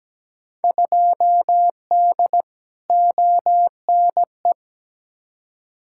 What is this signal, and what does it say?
Morse code, heard as a keyed tone, about 17 wpm, 700 Hz. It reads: 2D ONE
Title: Field Day Exchanges - 17wpm